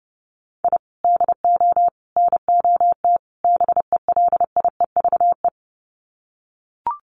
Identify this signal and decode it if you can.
Morse code, heard as a keyed tone, about 30 wpm, 700 Hz. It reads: I DO NOT BELIEVE
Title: Sets of 4 Words 30wpm